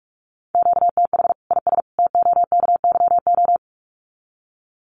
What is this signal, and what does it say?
Morse code, heard as a keyed tone, about 44 wpm, 700 Hz. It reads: QTH IS TOKYO
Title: Common QSO Phrases 45wpm